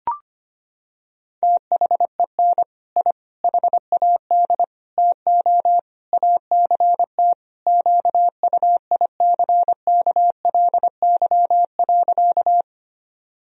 Morse code, heard as a keyed tone, about 25 wpm, 700 Hz. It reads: THEN I HAD TO ACT QUICKLY.
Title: Sentences from Top 1000 Words - Review Part 1 - 25wpm